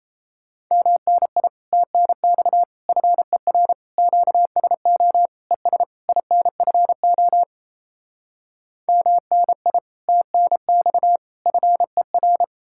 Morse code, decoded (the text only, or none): MNI TNX FER QSO ES INFO MNI TNX FER